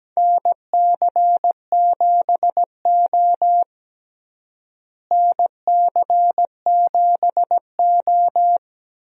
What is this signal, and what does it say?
Morse code, heard as a keyed tone, about 17 wpm, 700 Hz. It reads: NC7O NC7O
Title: USA Call Signs - 4 Characters - Repeated 3x - 17wpm